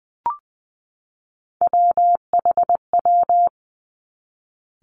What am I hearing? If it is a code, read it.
Morse code, heard as a keyed tone, about 20 wpm, 700 Hz. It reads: WHW
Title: Three Challenging Letter-Number Combinations 20wpm